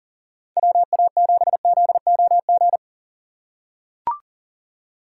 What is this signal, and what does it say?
Morse code, heard as a keyed tone, about 40 wpm, 700 Hz. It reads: WA7ZOG
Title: USA Call Signs - 6 Characters - 6x Character Spacing - 40wpm